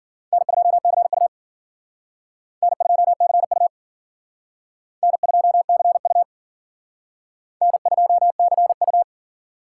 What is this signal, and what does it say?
Morse code, heard as a keyed tone, about 52 wpm, 700 Hz. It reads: N2CU N2CU N2CU N2CU